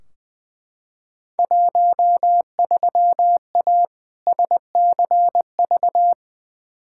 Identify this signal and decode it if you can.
Morse code, heard as a keyed tone, about 20 wpm, 700 Hz. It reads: 13A SCV
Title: Field Day Exchanges Encore - 20wpm